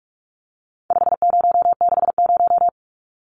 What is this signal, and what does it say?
Morse code, heard as a keyed tone, about 44 wpm, 700 Hz. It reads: 5060
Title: Numbers - 4-Digit 45wpm